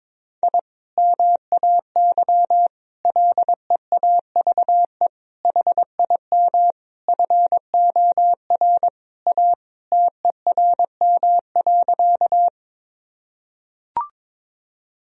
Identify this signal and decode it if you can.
Morse code, heard as a keyed tone, about 21 wpm, 700 Hz. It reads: I MAY LEAVE HIM FOR A TERM.
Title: Sentences from Top 1000 Words - Review Part 4 - 22wpm